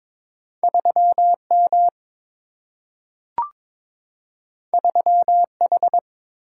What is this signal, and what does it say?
Morse code, heard as a keyed tone, about 22 wpm, 700 Hz. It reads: 3 M 3 H
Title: Single-Letter Single-Number 22wpm